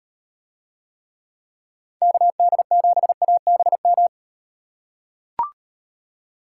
Morse code, heard as a keyed tone, about 37 wpm, 700 Hz. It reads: KD7ABM